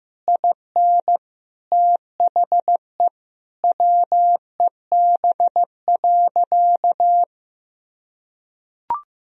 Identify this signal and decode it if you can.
Morse code, heard as a keyed tone, about 16 wpm, 700 Hz. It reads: IN THE WEB.
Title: Sentences from Top 1000 Words - Review Part 4 - 15wpm